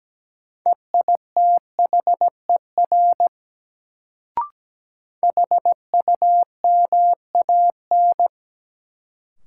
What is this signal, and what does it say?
Morse code, heard as a keyed tone, about 17 wpm, 700 Hz. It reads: EITHER HUMAN
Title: Sets of 5 Words - New Word Drill - 17wpm